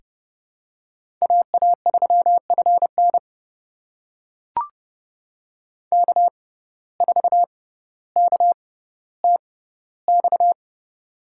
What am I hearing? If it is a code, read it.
Morse code, heard as a keyed tone, about 30 wpm, spaced as for 15 wpm, 700 Hz. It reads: AA3FN K4KTX
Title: USA Call Signs - 5 Characters - 6x Character Spacing - 30wpm